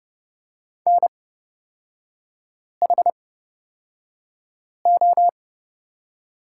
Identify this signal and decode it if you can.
Morse code, heard as a keyed tone, about 29 wpm, spaced as for 5 wpm, 700 Hz. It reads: NHO
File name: Lesson 36e - BT - Three Characters 30wpm